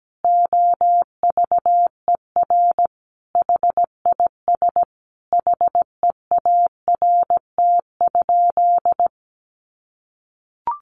Morse code, heard as a keyed tone, about 17 wpm, 700 Hz. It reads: OVER HIS HEART?
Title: Sentences from Top 1200 Words 17wpm